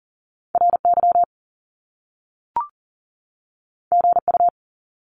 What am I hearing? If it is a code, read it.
Morse code, heard as a keyed tone, about 38 wpm, 700 Hz. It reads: RY GU
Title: Top Two Letter Combinations in Words 40wpm